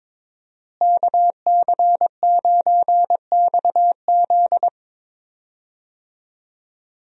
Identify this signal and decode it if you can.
Morse code, heard as a keyed tone, about 22 wpm, 700 Hz. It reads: KC9XZ
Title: USA Call Signs - 5 Characters - Repeated 2x - 22wpm